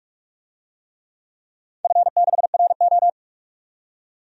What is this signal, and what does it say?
Morse code, heard as a keyed tone, about 44 wpm, 690 Hz. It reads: F6RO
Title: DXCC-100 Call Signs - 4 Characters - 45wpm